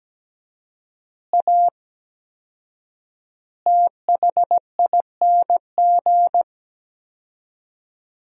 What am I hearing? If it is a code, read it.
Morse code, heard as a keyed tone, about 17 wpm, 700 Hz. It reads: A THING
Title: 4x Word-Spacing - Sentences from Top 300 Words - 17wpm